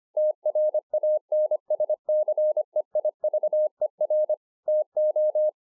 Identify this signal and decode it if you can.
Morse code, heard as a keyed tone, about 25 wpm, 610 Hz. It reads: TRANSCEIVER TO